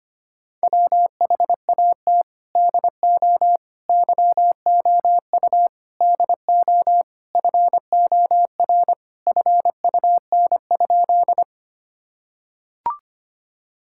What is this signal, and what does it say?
Morse code, heard as a keyed tone, about 25 wpm, 700 Hz. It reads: WHAT DO YOU DO FOR FUN?
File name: practice / 25wpm / Sentences with 4-Word Phrases - 25wpm